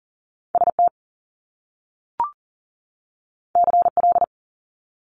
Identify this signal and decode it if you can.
Morse code, heard as a keyed tone, about 40 wpm, 700 Hz. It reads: ST CL
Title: Top Two Letter Combinations in Words 40wpm